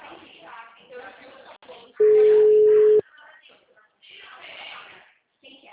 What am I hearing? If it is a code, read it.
Ringback tone (Europe, 425 Hz), with speech.